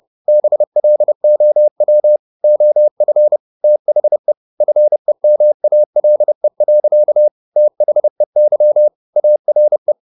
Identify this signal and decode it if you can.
Morse code, heard as a keyed tone, about 30 wpm, 600 Hz. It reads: BLOW OF THE FEMALE. THEY ARE
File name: the old man and the sea / the old man and the sea 1 30wpm